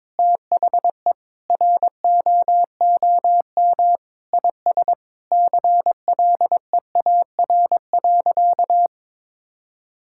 Morse code, heard as a keyed tone, about 22 wpm, 700 Hz. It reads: THE ROOM IS CLEAR.
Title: Sentences from Top 1000 Words - Review Part 6 - 22wpm